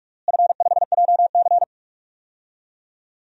Morse code, heard as a keyed tone, about 46 wpm, 700 Hz. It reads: F5JC
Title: DXCC-100 Call Signs - 4 Characters - Repeated 3x - 45wpm